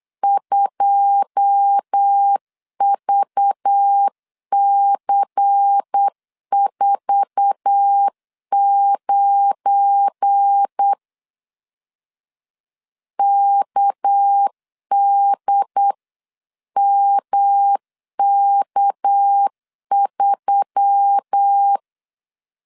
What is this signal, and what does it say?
Morse code, heard as a keyed tone, about 8 wpm, 800 Hz. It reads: 2VC49 KDMK3
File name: CW / DET / 1 9 1500C S C 8wpm 5653